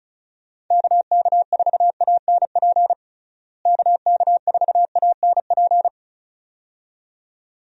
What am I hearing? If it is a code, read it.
Morse code, heard as a keyed tone, about 36 wpm, 700 Hz. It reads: KK4ANP KK4ANP
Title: USA Call Signs - 6 Characters - Repeated 2x - 35wpm